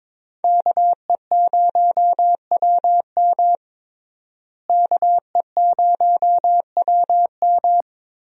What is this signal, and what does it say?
Morse code, heard as a keyed tone, about 22 wpm, 700 Hz. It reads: KE0WM KE0WM
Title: USA Call Signs - 5 Characters - Repeated 3x - 22wpm